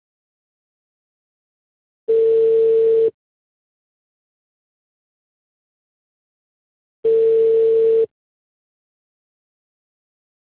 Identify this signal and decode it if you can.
Ringback tone (China, 450 Hz).